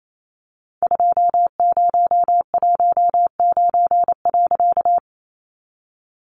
Morse code, heard as a keyed tone, about 28 wpm, 700 Hz. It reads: 2019.
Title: A Sound Of Thunder 28wpm